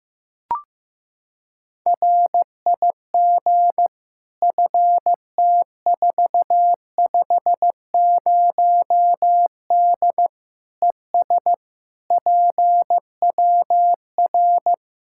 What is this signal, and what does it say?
Morse code, heard as a keyed tone, about 15 wpm, 700 Hz. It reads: RIG FT450D ES PWR
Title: K1USN SST QSO Practice 15wpm